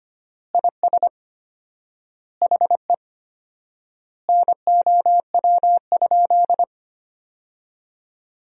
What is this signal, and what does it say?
Morse code, heard as a keyed tone, about 25 wpm, 700 Hz. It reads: IS HE NOW?
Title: 4x Word-Spacing - Sentences from Top 200 Words - 25wpm